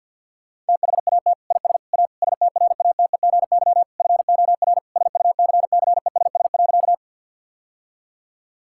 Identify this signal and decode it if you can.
Morse code, heard as a keyed tone, about 50 wpm, 700 Hz. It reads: THAT IS A STRATEGY FOR SUCCESS.
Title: Sentences from Top 1000 Words - Review Part 5 - 50wpm